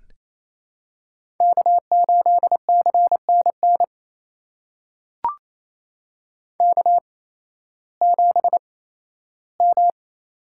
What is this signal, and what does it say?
Morse code, heard as a keyed tone, about 28 wpm, spaced as for 15 wpm, 700 Hz. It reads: K8CNN K7M
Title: USA Call Signs - 5 Characters - 8x Character Spacing - 28wpm